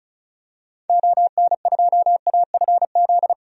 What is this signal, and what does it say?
Morse code, heard as a keyed tone, about 36 wpm, 700 Hz. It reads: ON2AFZ
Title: DXCC-100 Call Signs - 6 Characters - 35wpm